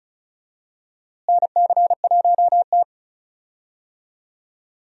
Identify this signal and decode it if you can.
Morse code, heard as a keyed tone, about 36 wpm, 700 Hz. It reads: NC1T
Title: USA Call Signs - 4 Characters - Repeated 3x - 35wpm